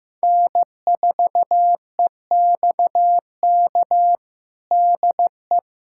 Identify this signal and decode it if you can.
Morse code, heard as a keyed tone, about 15 wpm, 700 Hz. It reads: N4EXK DE
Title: USA Callsign DE Callsign - 5 Characters - 15wpm